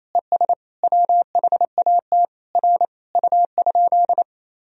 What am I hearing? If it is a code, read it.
Morse code, heard as a keyed tone, about 28 wpm, 700 Hz. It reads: ES WHAT R U?
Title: Sentences from Top 100 Words and Abbreviations 28wpm